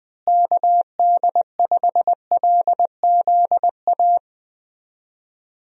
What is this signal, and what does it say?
Morse code, heard as a keyed tone, about 20 wpm, 700 Hz. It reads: KD5LZA
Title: USA Call Signs - 6 Characters - Repeated 3x - 20wpm